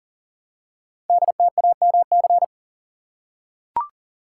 Morse code, heard as a keyed tone, about 38 wpm, 700 Hz. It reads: DTAMC
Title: Code Groups - 5 Letters - 40wpm